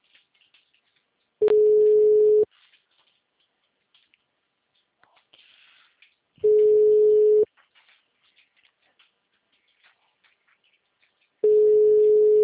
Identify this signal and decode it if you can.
Ringback tone (Europe, 425 Hz).